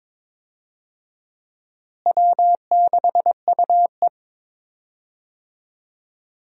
Morse code, heard as a keyed tone, about 21 wpm, 700 Hz. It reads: W6UE
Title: USA Call Signs - 4 Characters - Repeated 3x - 22wpm